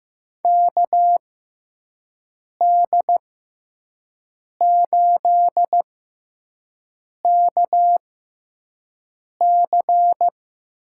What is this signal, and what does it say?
Morse code, heard as a keyed tone, about 15 wpm, spaced as for 5 wpm, 700 Hz. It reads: KD8KC
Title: USA Call Signs - 6 Characters - 6x Character Spacing - 15wpm